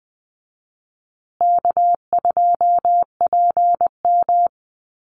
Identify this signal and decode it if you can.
Morse code, heard as a keyed tone, about 20 wpm, 700 Hz. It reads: K2PM
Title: USA Call Signs Repeated 3x 20wpm